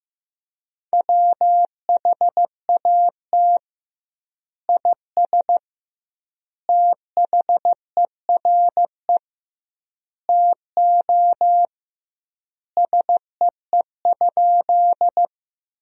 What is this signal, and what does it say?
Morse code, heard as a keyed tone, about 15 wpm, 700 Hz. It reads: WHAT IS THERE TO SEE?
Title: 2x Word-Spacing - Sentences from Top 100 Words - 15wpm